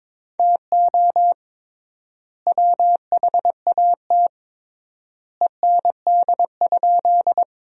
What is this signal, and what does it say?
Morse code, heard as a keyed tone, about 22 wpm, 700 Hz. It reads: TO WHAT END?
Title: 3x Word-Spacing - Sentences from Top 200 Words - 22wpm